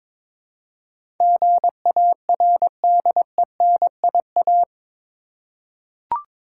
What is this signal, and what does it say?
Morse code, heard as a keyed tone, about 22 wpm, 700 Hz. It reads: GARDENIA
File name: Difficult Words with Hints - 22wpm